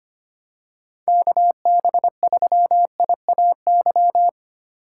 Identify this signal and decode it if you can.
Morse code, heard as a keyed tone, about 25 wpm, 700 Hz. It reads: KB3IAY
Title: USA Call Signs - 3x Character Spacing - 25wpm